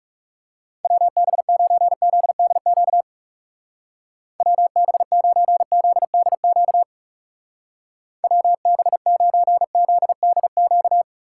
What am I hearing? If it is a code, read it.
Morse code, heard as a keyed tone, about 38 wpm, 690 Hz. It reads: WB9ZDQ WB9ZDQ WB9ZDQ